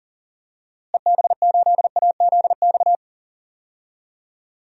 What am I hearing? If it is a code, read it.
Morse code, heard as a keyed tone, about 37 wpm, 700 Hz. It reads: EB8AZX
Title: DXCC-100 Call Signs - 6 Characters - Repeated 2x - 40wpm